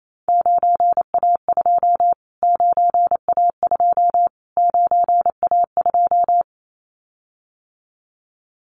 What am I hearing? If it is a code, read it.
Morse code, heard as a keyed tone, about 28 wpm, 700 Hz. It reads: 9A2 9A2 9A2
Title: Callsign Prefixes for Top 25 DXCC - Repeated 3x - 28wpm